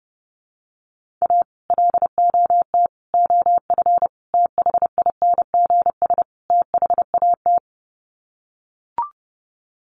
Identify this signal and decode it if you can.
Morse code, heard as a keyed tone, about 30 wpm, 700 Hz. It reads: A LOT OF THINGS THAT
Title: Sets of 5 Words 30wpm